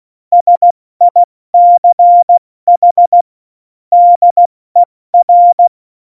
Morse code, heard as a keyed tone, about 16 wpm, spaced as for 14 wpm, 700 Hz. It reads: SICH DER
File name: Morsekurs / Witze / Witz 06 March 2018, 70 F80